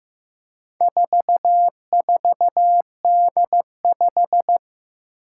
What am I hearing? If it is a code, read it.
Morse code, heard as a keyed tone, about 15 wpm, 700 Hz. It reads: 44D5
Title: Four Challenging Letter-Number Combinations 15wpm